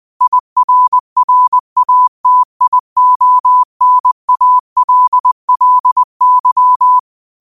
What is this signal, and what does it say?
Morse code, heard as a keyed tone, about 20 wpm, 1000 Hz. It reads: IRRATIONALLY
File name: qotdfiles / 20wpm